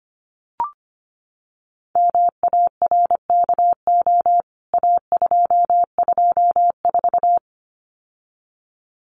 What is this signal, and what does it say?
Morse code, heard as a keyed tone, about 25 wpm, 700 Hz. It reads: MARKO A224
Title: CWT Exchange 25wpm